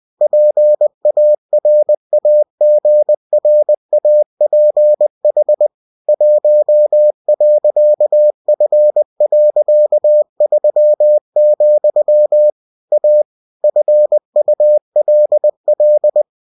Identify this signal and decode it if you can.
Morse code, heard as a keyed tone, about 20 wpm, 600 Hz. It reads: PARAGRAPH 1.F.3, A FULL
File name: cw-books / winnie-the-pooh / winnie-20wpm-page-0117